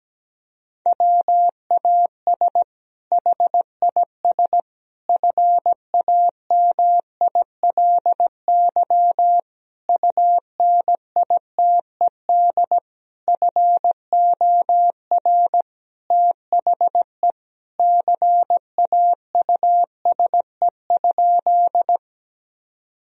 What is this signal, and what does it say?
Morse code, heard as a keyed tone, about 17 wpm, 700 Hz. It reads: WAS HIS FAMILY UNITED FOR THE CAUSE?
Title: Sentences from Top 1000 Words - Review Part 4 - 17wpm